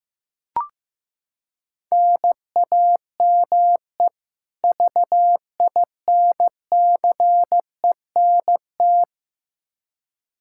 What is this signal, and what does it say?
Morse code, heard as a keyed tone, about 15 wpm, 700 Hz. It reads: NAME VINCENT
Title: Practice QSO - Exchange 1 - Beginning - 15wpm